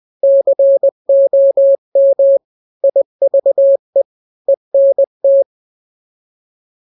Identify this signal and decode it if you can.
Morse code, heard as a keyed tone, about 20 wpm, 550 Hz. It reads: COM IVE ENT
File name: n-grams / 3-grams 20 18 WPM